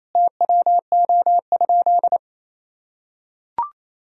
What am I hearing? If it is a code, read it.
Morse code, heard as a keyed tone, about 28 wpm, 700 Hz. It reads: TWO?
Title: Sentences from Top 1000 Words - Review Part 4 - 28wpm